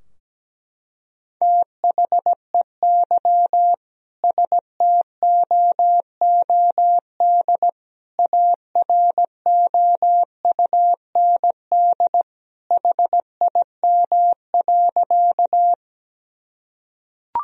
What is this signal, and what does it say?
Morse code, heard as a keyed tone, about 17 wpm, 700 Hz. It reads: THEY STOOD AROUND HIM.
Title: Sentences from Top 1000 Words - Review Part 3 - 17wpm